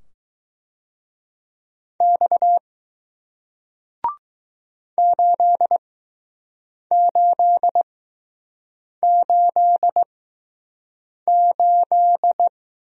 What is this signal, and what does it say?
Morse code, heard as a keyed tone, about 20 wpm, spaced as for 6 wpm, 700 Hz. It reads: X 8888